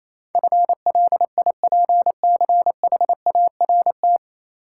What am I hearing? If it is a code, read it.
Morse code, heard as a keyed tone, about 27 wpm, 700 Hz. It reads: FLIPCHART
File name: Easy Words - Speed-Racing with Hints - 22wpm